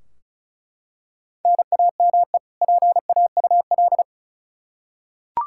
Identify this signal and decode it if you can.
Morse code, heard as a keyed tone, about 36 wpm, 700 Hz. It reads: NAME PAUL
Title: Practice QSO - Exchange 1 - Beginning - 35wpm